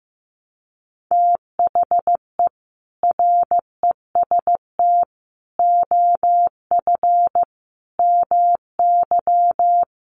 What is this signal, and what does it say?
Morse code, heard as a keyed tone, about 15 wpm, 700 Hz. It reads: THE REST OF MY
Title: Sets of 5 Words 15wpm